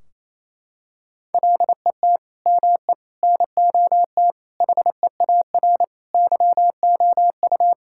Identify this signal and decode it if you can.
Morse code, heard as a keyed tone, about 28 wpm, 700 Hz. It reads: LET ME NOT HEAR YOU
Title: Sentences from Top 1600 Words 28wpm